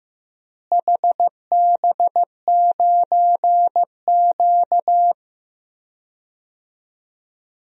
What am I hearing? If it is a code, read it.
Morse code, heard as a keyed tone, about 15 wpm, 700 Hz. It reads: HB9Q
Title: DXCC-100 Call Signs - 4 Characters - Repeated 2x - 15wpm